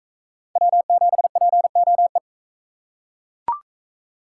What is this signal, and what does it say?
Morse code, heard as a keyed tone, about 41 wpm, 700 Hz. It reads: W7POE